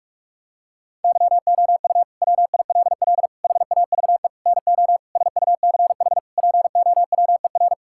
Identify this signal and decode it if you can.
Morse code, heard as a keyed tone, about 46 wpm, 700 Hz. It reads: YOU WILL HAVE NO SUCH POWER
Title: Sentences from Top 1000 Words - Review Part 6 - 45wpm